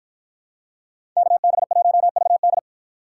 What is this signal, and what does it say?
Morse code, heard as a keyed tone, about 53 wpm, 700 Hz. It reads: KB1VD